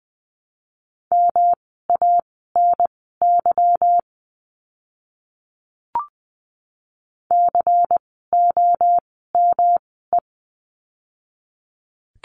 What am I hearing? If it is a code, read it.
Morse code, heard as a keyed tone, about 20 wpm, spaced as for 15 wpm, 700 Hz. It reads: MANY COME
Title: 230 Lesson 37f - Letter x - Words 20wpm